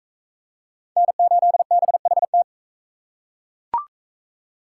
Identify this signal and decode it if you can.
Morse code, heard as a keyed tone, about 39 wpm, 700 Hz. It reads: N8BHT